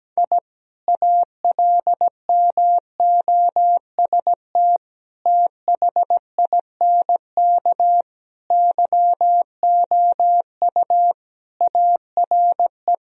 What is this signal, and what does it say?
Morse code, heard as a keyed tone, about 17 wpm, 700 Hz. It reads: I ALMOST THINK YOU ARE
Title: Sentences from Top 1000 Words - Review Part 4 - 17wpm